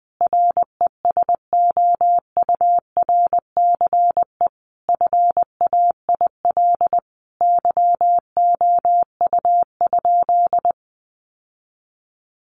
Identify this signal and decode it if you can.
Morse code, heard as a keyed tone, about 20 wpm, 700 Hz. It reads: RESOURCE FAIL YOU?
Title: Sentences from Top 1300 Words 20wpm